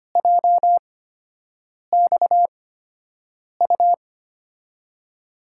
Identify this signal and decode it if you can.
Morse code, heard as a keyed tone, about 25 wpm, spaced as for 7 wpm, 700 Hz. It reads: JXU
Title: USA Call Signs - 8x Character Spacing - 25wpm